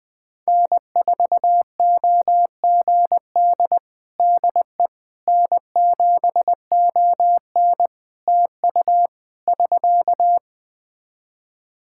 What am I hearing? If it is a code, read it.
Morse code, heard as a keyed tone, about 20 wpm, 700 Hz. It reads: N4OGD DE N7ON TU <SK>
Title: Practice QSO - Exchange 4 - Ending - 20wpm